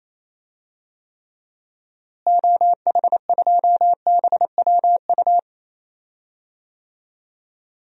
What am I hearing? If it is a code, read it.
Morse code, heard as a keyed tone, about 28 wpm, 700 Hz. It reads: OH2BWU
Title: DXCC-100 Call Signs - 6 Characters - Repeated 2x - 28wpm